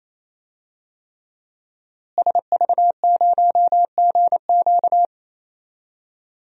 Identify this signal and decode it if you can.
Morse code, heard as a keyed tone, about 28 wpm, 700 Hz. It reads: SV0GQ
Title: DXCC-100 Call Signs - 5 Characters - Repeated 3x - 28wpm